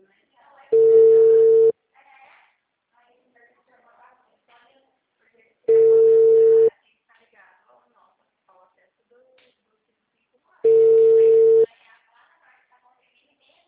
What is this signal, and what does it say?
Ringback tone (China, 450 Hz), with speech.